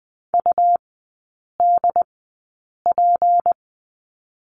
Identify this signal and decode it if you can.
Morse code, heard as a keyed tone, about 20 wpm, spaced as for 8 wpm, 700 Hz. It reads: UDP
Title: 152 Lesson 25e - Letter p - Three Characters 20wpm